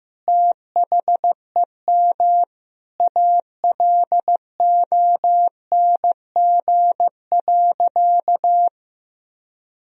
Morse code, heard as a keyed tone, about 15 wpm, 700 Hz. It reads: THEM ALONG.
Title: Sentences from Top 1000 Words - Review Part 3 - 15wpm